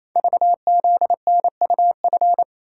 Morse code, heard as a keyed tone, about 28 wpm, 700 Hz. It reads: VZNUF
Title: Code Groups - 5 Letter:Numbers - 28wpm